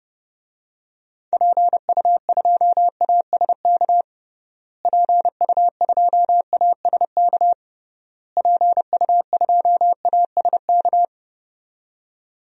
Morse code, heard as a keyed tone, about 30 wpm, 700 Hz. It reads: PU2ASK PU2ASK PU2ASK
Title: DXCC-100 Call Signs - 6 Characters - Repeated 3x - 30wpm